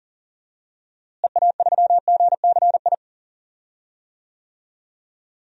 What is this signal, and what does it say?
Morse code, heard as a keyed tone, about 37 wpm, 700 Hz. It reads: EA3GCI
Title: DXCC-100 Call Signs - 6 Characters - Repeated 2x - 40wpm